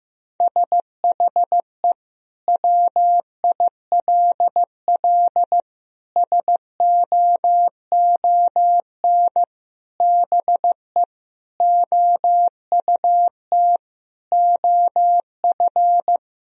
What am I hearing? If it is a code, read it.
Morse code, heard as a keyed tone, about 15 wpm, 700 Hz. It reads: SHE WILL SOON BE OUT OF
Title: Sentences from Top 1000 Words - Review Part 2 - 15wpm